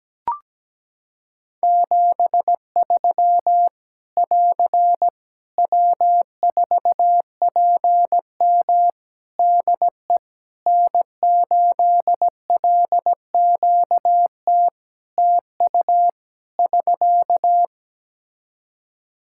Morse code, heard as a keyed tone, about 17 wpm, 700 Hz. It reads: 73 <AR> W4PM DE N8LQT TU <SK>
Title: Practice QSO - Exchange 4 - Ending - 17wpm